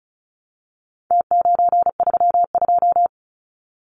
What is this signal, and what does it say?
Morse code, heard as a keyed tone, about 34 wpm, 700 Hz. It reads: T932
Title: Numbers - Contest Style - 4-Digit 35wpm